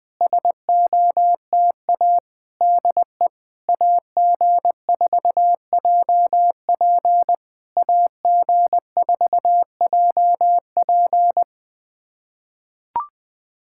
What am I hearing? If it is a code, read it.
Morse code, heard as a keyed tone, about 20 wpm, 700 Hz. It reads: SOTA DE AG4JP AG4JP
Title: SOTA QSO Practice 20wpm